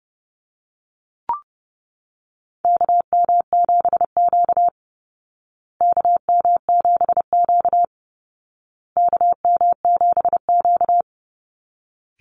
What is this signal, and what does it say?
Morse code, heard as a keyed tone, about 30 wpm, 700 Hz. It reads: KM7Q KM7Q KM7Q
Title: Call Signs Repeated 2x - 30wpm